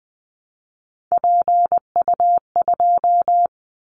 Morse code, heard as a keyed tone, about 20 wpm, 700 Hz. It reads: PU2
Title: Callsign Prefixes for Top 100 DXCC - 20wpm